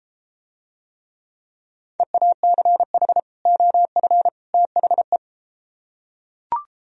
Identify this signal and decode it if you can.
Morse code, heard as a keyed tone, about 34 wpm, 700 Hz. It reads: EACH OF THE